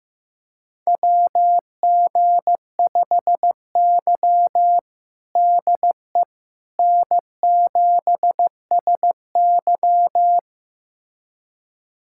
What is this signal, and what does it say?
Morse code, heard as a keyed tone, about 15 wpm, 700 Hz. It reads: WG5Y DE N7SY
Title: USA Callsign DE Callsign - 4 Characters - 15wpm